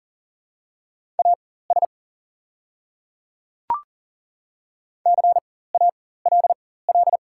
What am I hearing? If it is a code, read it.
Morse code, heard as a keyed tone, about 40 wpm, spaced as for 19 wpm, 700 Hz. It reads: AS CALL
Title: 4x Character Spacing - Top 300 Words - 40wpm